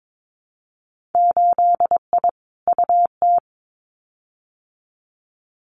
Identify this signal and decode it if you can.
Morse code, heard as a keyed tone, about 22 wpm, 700 Hz. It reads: 8I UT
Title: Winter Field Day Exchanges Encore 22wpm